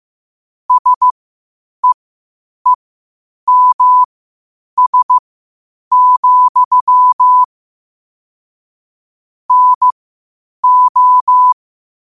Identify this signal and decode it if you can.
Morse code, heard as a keyed tone, about 15 wpm, spaced as for 8 wpm, 1000 Hz. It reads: SEEMS, NO
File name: qotdfiles / 8wpm